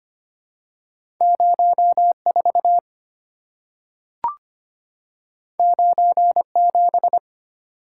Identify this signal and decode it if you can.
Morse code, heard as a keyed tone, about 25 wpm, 700 Hz. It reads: 04 97
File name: Two Letter-Number Combinations - 25wpm